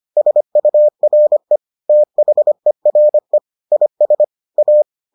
Morse code, heard as a keyed tone, about 25 wpm, 600 Hz. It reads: SURE THERE IS A